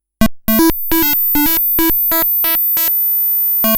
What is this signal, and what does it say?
There is no code.